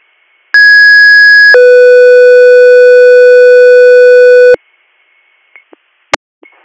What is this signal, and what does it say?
Two-tone page, heard as a one-second tone, then a three-second tone, about 1650 then 500 Hz. It is heard from 0.5 to 4.6 s.